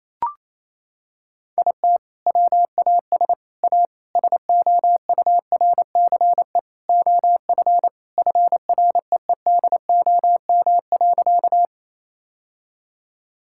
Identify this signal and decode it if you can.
Morse code, heard as a keyed tone, about 28 wpm, 700 Hz. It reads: IT WAS A SOURCE OF FREEDOM.
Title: Sentences from Top 1000 Words - Review Part 5 - 28wpm